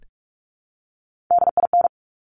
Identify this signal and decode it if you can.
Morse code, heard as a keyed tone, about 46 wpm, 700 Hz. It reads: DIN
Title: Top Three Letter Combinations in Words 45wpm